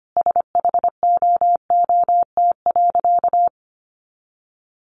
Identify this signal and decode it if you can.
Morse code, heard as a keyed tone, about 25 wpm, 700 Hz. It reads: SHOOT.
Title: Sentences from Top 1200 Words 25wpm